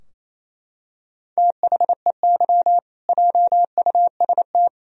Morse code, heard as a keyed tone, about 27 wpm, 700 Hz. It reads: THEY JUST
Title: Sentences from Top 1000 Words - Review Part 3 - 28wpm